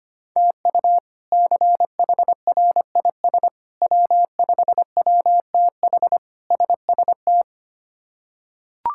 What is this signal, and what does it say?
Morse code, heard as a keyed tone, about 25 wpm, 700 Hz. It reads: TU CHRIS W5WTH SST
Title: K1USN SST QSO Practice - 25wpm